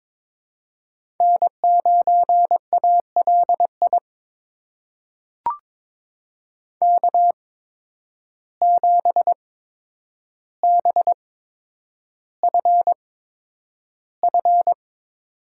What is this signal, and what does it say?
Morse code, heard as a keyed tone, about 22 wpm, spaced as for 9 wpm, 700 Hz. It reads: N9ALI K7BFF
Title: USA Call Signs - 5 Characters - 8x Character Spacing - 22wpm